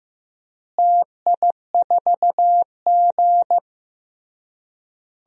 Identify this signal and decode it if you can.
Morse code, heard as a keyed tone, about 15 wpm, 700 Hz. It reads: TI4G
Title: DXCC-100 Call Signs - 4 Characters - Repeated 3x - 15wpm